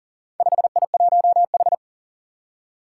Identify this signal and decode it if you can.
Morse code, heard as a keyed tone, about 40 wpm, 700 Hz. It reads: 5I1H
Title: Four Challenging Letter-Number Combinations 40wpm